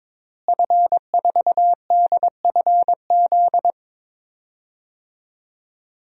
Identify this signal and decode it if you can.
Morse code, heard as a keyed tone, about 22 wpm, 700 Hz. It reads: F4DFZ
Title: DXCC-100 Call Signs - 5 Characters - Repeated 2x - 22wpm